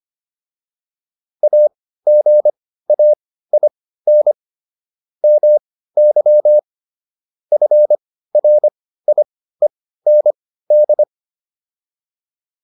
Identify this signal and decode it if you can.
Morse code, heard as a keyed tone, about 25 wpm, spaced as for 15 wpm, 600 Hz. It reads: AGAIN MY FRIEND